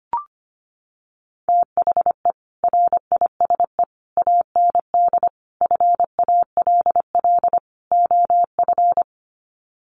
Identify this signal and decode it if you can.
Morse code, heard as a keyed tone, about 25 wpm, 700 Hz. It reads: THE RISE AND FALL OF
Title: Sets of 5 Words Encore2 25wpm